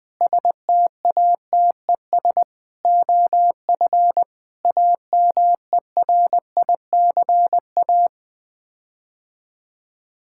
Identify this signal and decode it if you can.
Morse code, heard as a keyed tone, about 20 wpm, 700 Hz. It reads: STATES OF AMERICA
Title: Sets of 8 Words - 20wpm